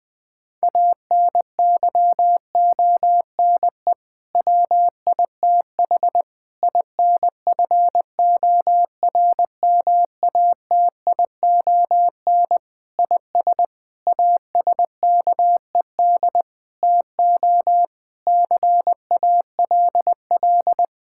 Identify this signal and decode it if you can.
Morse code, heard as a keyed tone, about 21 wpm, 700 Hz. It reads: ANYONE WITH INFORMATION IS ASKED TO CALL
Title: Sentences with 6-Word Phrases - 20wpm